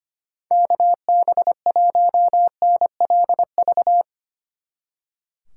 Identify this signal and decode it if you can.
Morse code, heard as a keyed tone, about 24 wpm, 700 Hz. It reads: KB1NLV
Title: Speed Racing - USA Call Signs 25wpm